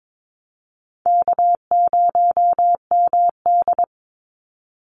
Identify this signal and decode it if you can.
Morse code, heard as a keyed tone, about 22 wpm, 700 Hz. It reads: K0MD
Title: Easy Call Signs - 22wpm